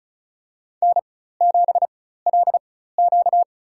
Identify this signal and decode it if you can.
Morse code, heard as a keyed tone, about 36 wpm, spaced as for 16 wpm, 700 Hz. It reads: N7LQ
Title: USA Call Signs - 4 Characters - 4x Character Spacing - 35wpm